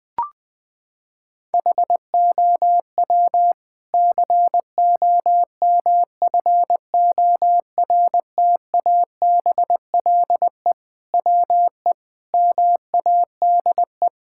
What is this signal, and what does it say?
Morse code, heard as a keyed tone, about 20 wpm, 700 Hz. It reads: HOW COMFORTABLE WE MADE
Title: Sentences from Top 1600 Words 20wpm